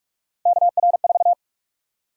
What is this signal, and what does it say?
Morse code, heard as a keyed tone, about 46 wpm, 700 Hz. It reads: KR4
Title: USA Call Sign Prefixes - Repeated 2x - 45wpm